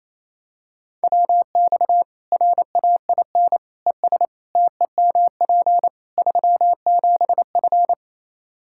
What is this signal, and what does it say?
Morse code, heard as a keyed tone, about 28 wpm, 700 Hz. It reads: WX RAIN ES TEMP 37F
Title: Practice QSO - Exchange 2 - Info - 28wpm